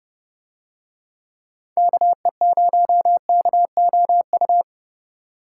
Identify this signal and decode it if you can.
Morse code, heard as a keyed tone, about 28 wpm, 700 Hz. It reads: KE0KOU